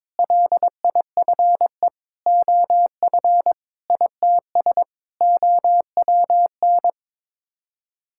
Morse code, heard as a keyed tone, about 22 wpm, 700 Hz. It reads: LIFE OF ITS OWN
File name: practice / 22wpm / Sets of 7 Words - 22wpm